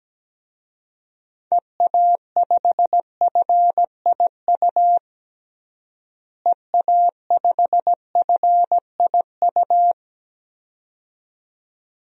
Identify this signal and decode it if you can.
Morse code, heard as a keyed tone, about 17 wpm, 700 Hz. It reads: EA5FIU EA5FIU
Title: DXCC-100 Call Signs - 6 Characters - Repeated 2x - 17wpm